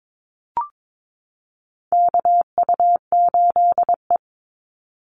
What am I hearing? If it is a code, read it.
Morse code, heard as a keyed tone, about 22 wpm, 700 Hz. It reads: KU8E
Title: Call Signs Repeated 3x - 22wpm